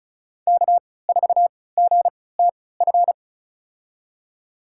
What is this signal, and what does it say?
Morse code, heard as a keyed tone, about 36 wpm, spaced as for 20 wpm, 700 Hz. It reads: K4GTF
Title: USA Call Signs - 5 Characters - 3x Character Spacing - 35wpm